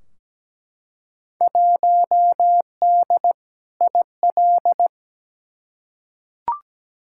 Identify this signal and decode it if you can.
Morse code, heard as a keyed tone, about 17 wpm, 700 Hz. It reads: 1D IL
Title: Field Day Exchanges - 17wpm